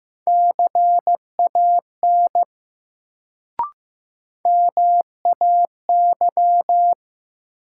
Morse code, heard as a keyed tone, about 15 wpm, 700 Hz. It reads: CAN MAY
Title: Sentences from Top 100 Words - New Word Drill - 15wpm